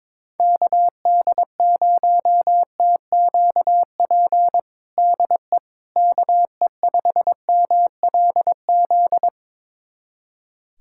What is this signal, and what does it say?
Morse code, heard as a keyed tone, about 22 wpm, 700 Hz. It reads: KD0TQP DE KE5MLZ
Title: USA Callsign DE Callsign - 6 Characters - 22wpm